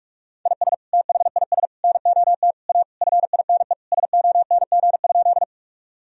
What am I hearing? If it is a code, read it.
Morse code, heard as a keyed tone, about 46 wpm, 700 Hz. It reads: IS THIS NOT A FINE SONG?
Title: Sentences from Top 1000 Words - Review Part 2 - 45wpm